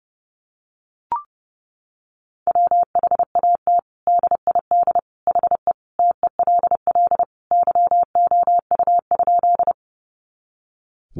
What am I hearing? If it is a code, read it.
Morse code, heard as a keyed tone, about 29 wpm, 700 Hz. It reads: WHAT DID HE TELL YOU?
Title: Sentences from Top 200 Words 30wpm